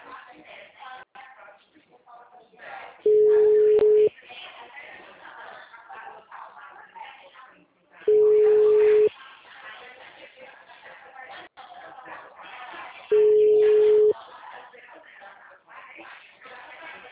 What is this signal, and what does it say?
Ringback tone (Europe, 425 Hz), with speech.